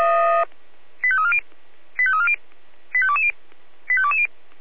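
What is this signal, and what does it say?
Five-tone selective call (ZVEI), heard as quick runs of five tones, tones of about 70 ms. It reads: 85340 85340 85220 85220
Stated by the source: ZVEI five-tone selective call received on an SDR.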